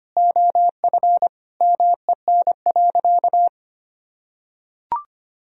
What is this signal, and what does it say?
Morse code, heard as a keyed tone, about 25 wpm, 700 Hz. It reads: OF MEN.
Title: Sentences from Top 1000 Words - Review Part 1 - 25wpm